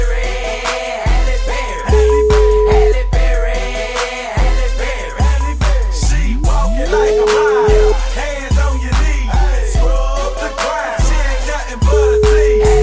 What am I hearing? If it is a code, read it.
Ringback tone (Europe, 425 Hz), with noise.